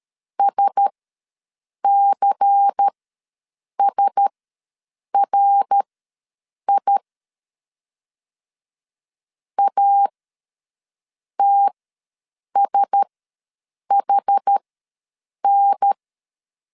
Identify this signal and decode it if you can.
Morse code, heard as a keyed tone, about 13 wpm, spaced as for 7 wpm, 800 Hz. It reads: SCSRI ATSHN
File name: LEZIONI / 5 LBVC / 5 LBVC ws 12wpm 3159